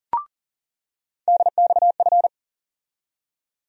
Morse code, heard as a keyed tone, about 40 wpm, 700 Hz. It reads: DXF
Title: USA Call Sign Postfixes - 3 Letters 40wpm